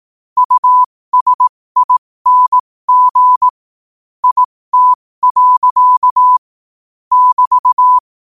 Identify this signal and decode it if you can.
Morse code, heard as a keyed tone, about 18 wpm, spaced as for 15 wpm, 1000 Hz. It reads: USING IT. <BT>
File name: qotdfiles / 15wpm